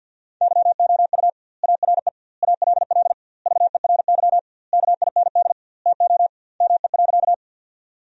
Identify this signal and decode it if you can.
Morse code, heard as a keyed tone, about 50 wpm, 700 Hz. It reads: YOU ARE ALL VERY KIND TO ME.
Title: 2x Word-Spacing - Sentences from Top 200 Words - 50wpm